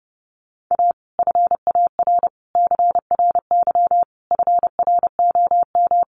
Morse code, heard as a keyed tone, about 30 wpm, 700 Hz. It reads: A FAR CRY FROM
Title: Sets of 5 Words Encore 30wpm